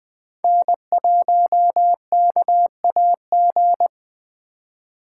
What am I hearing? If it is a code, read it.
Morse code, heard as a keyed tone, about 20 wpm, 700 Hz. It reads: N1KAG
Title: Speed Racing - USA Call Signs 17wpm